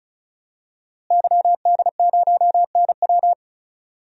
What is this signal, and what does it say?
Morse code, heard as a keyed tone, about 36 wpm, 700 Hz. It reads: YD0NW
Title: DXCC-100 Call Signs - 5 Characters - Repeated 3x - 35wpm